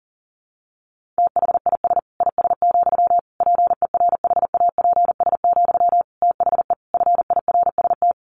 Morse code, heard as a keyed tone, about 37 wpm, 700 Hz. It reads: THIS IS, PERHAPS, THE FIRST
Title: Sentences from Top 1500 Words 40wpm